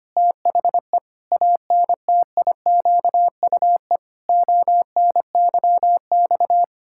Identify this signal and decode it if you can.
Morse code, heard as a keyed tone, about 25 wpm, 700 Hz. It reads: THE ANTIQUE ONYX
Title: Pangram Challenge v2 - 25wpm